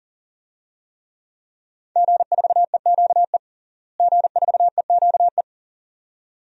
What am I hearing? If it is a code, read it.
Morse code, heard as a keyed tone, about 40 wpm, 700 Hz. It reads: G4EQE G4EQE
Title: DXCC-100 Call Signs - 5 Characters - Repeated 2x - 40wpm